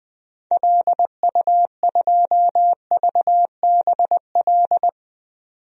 Morse code, heard as a keyed tone, about 20 wpm, 700 Hz. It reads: LU2VBL
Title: DXCC-100 Call Signs - 6 Characters - 20wpm